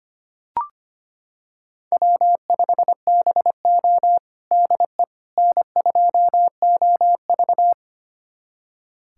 Morse code, heard as a keyed tone, about 25 wpm, 700 Hz. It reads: W5BO DE N2OV
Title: USA Callsign DE Callsign - 4 Characters - 25wpm